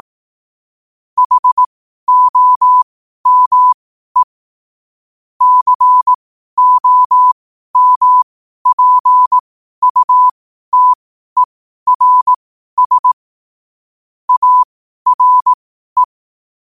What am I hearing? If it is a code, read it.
Morse code, heard as a keyed tone, about 18 wpm, spaced as for 12 wpm, 1000 Hz. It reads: HOME COMPUTERS ARE
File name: qotdfiles / 12wpm